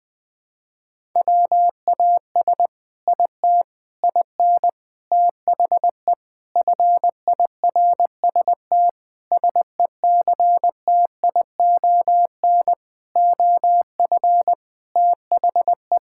Morse code, heard as a keyed tone, about 19 wpm, 700 Hz. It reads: WAS IT IN THE FIRST SECTION OF THE
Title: Sentences from Top 1000 Words - Review Part 5 - 20wpm